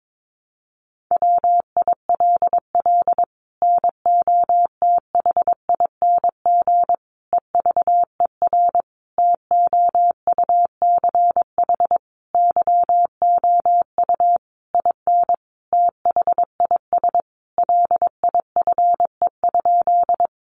Morse code, heard as a keyed tone, about 22 wpm, 700 Hz. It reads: WILL NOTHING EVER TOUCH YOU IN THIS LIFE?
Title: Sentences from Top 1200 Words 22wpm